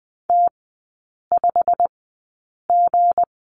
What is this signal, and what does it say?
Morse code, heard as a keyed tone, about 20 wpm, spaced as for 8 wpm, 690 Hz. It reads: T5G
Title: 159 Lesson 26e - Letter g - Three Characters 20wpm